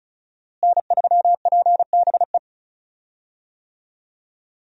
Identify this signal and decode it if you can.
Morse code, heard as a keyed tone, about 36 wpm, 700 Hz. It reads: N3PBE
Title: USA Call Signs - 5 Characters - Repeated 3x - 35wpm